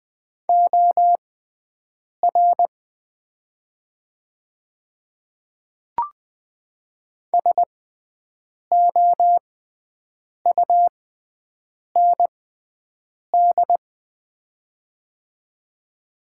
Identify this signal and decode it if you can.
Morse code, heard as a keyed tone, about 20 wpm, spaced as for 7 wpm, 700 Hz. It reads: OR SOUND
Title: 6x Character Spacing - Top 100 Words - 20wpm